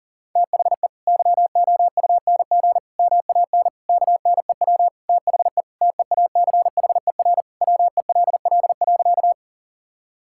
Morse code, heard as a keyed tone, about 40 wpm, 700 Hz. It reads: THE YOUNG MAN KNEW THE TEACHER WELL.
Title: Sentences from Top 1000 Words - Review Part 6 - 40wpm